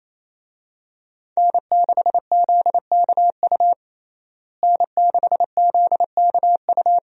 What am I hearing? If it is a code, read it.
Morse code, heard as a keyed tone, about 28 wpm, 700 Hz. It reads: N6ZKU N6ZKU
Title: USA Call Signs - 5 Characters - Repeated 2x - 28wpm